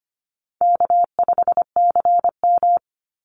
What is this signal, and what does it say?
Morse code, heard as a keyed tone, about 25 wpm, 700 Hz. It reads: K5CM
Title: Easy Call Signs Repeated 2x - 25wpm